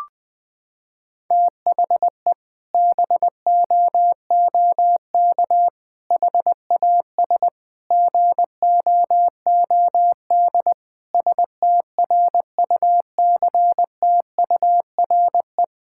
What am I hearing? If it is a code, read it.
Morse code, heard as a keyed tone, about 20 wpm, 700 Hz. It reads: THE BOOK HAS GOOD STRUCTURE
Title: Sentences from Top 1000 Words - Review Part 6 - 20wpm